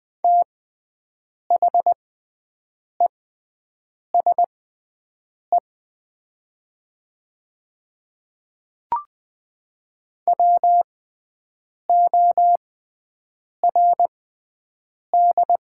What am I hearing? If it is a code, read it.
Morse code, heard as a keyed tone, about 20 wpm, spaced as for 7 wpm, 700 Hz. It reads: THESE WORD
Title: 6x Character Spacing - Top 100 Words - 20wpm